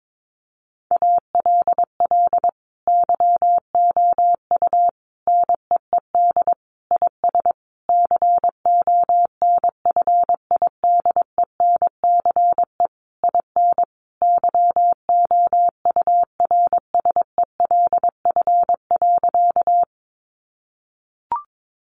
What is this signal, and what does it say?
Morse code, heard as a keyed tone, about 22 wpm, 700 Hz. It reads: ALL YOU NEED IS CONFIDENCE IN YOURSELF.
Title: Sentences from Top 1500 Words 22wpm